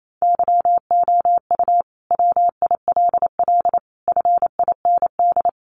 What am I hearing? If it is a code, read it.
Morse code, heard as a keyed tone, about 28 wpm, 700 Hz. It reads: YOU WILL FIND
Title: Sentences from Top 1100 Words 28wpm